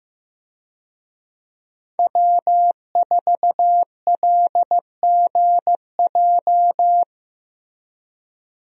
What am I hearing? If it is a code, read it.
Morse code, heard as a keyed tone, about 15 wpm, 700 Hz. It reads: W4LGJ
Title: USA Call Signs - 5 Characters - Repeated 2x - 15wpm